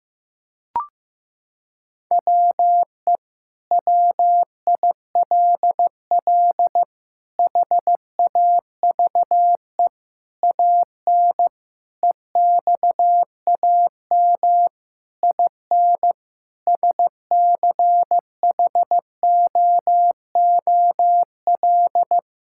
Morse code, heard as a keyed tone, about 16 wpm, 700 Hz. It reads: WE WILL HAVE AN EXAM IN SCHOOL
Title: Sentences from Top 1000 Words - Review Part 5 - 15wpm